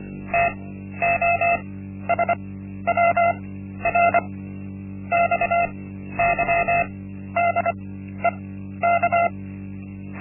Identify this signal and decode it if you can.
Morse code, heard as a keyed tone, about 24 wpm, spaced as for 12 wpm, 690 Hz. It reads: TOSWR XYDEK
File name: Linksmakalnis-1